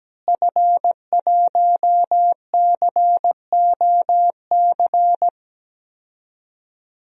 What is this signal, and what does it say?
Morse code, heard as a keyed tone, about 17 wpm, 700 Hz. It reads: F1COC
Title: DXCC-100 Call Signs - 5 Characters - Repeated 2x - 17wpm